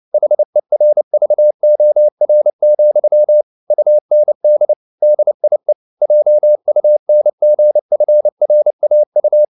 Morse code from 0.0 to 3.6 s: HERVOR,